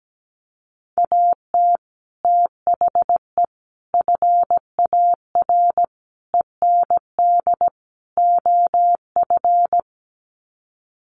Morse code, heard as a keyed tone, about 17 wpm, 700 Hz. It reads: AT THE FAR END OF
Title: Sets of 5 Words 17wpm